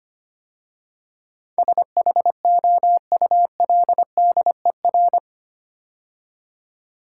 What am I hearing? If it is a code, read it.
Morse code, heard as a keyed tone, about 25 wpm, 690 Hz. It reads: SHOULDER
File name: 8 Letter Words - First 3 Letters Given - 25wpm